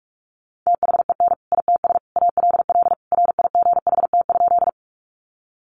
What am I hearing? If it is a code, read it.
Morse code, heard as a keyed tone, about 44 wpm, 700 Hz. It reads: THEN ITS ALL RIGHT?
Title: Sentences from Top 600 Words 45wpm